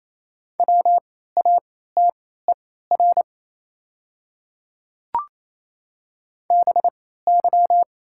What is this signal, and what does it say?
Morse code, heard as a keyed tone, about 28 wpm, spaced as for 16 wpm, 700 Hz. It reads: WATER BY
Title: 3x Character Spacing - Top 100 Words - 28wpm